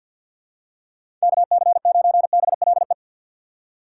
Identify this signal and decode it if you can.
Morse code, heard as a keyed tone, about 47 wpm, 700 Hz. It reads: KC9BLE
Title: USA Call Signs - 6 Characters - 3x Character Spacing - 50wpm